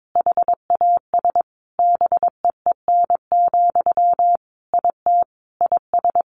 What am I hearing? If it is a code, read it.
Morse code, heard as a keyed tone, about 22 wpm, 700 Hz. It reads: HAS BEEN, IT IS
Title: Sentences from Top 1100 Words 22wpm